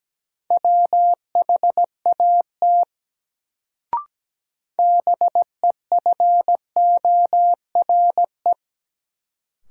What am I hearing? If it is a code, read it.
Morse code, heard as a keyed tone, about 17 wpm, 700 Hz. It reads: WHAT BEFORE
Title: Sets of 2 Words - New Word Drill - 17wpm